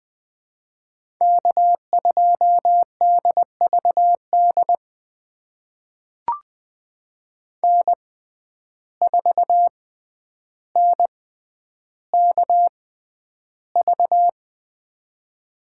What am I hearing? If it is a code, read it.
Morse code, heard as a keyed tone, about 20 wpm, spaced as for 10 wpm, 700 Hz. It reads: K2DVD N4NKV
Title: USA Call Signs - 5 Characters - 6x Character Spacing - 20wpm